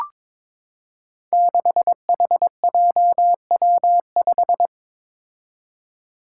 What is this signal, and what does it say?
Morse code, heard as a keyed tone, about 22 wpm, 700 Hz. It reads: 6HJW5
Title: Five Challenging Letter-Number Combinations 22wpm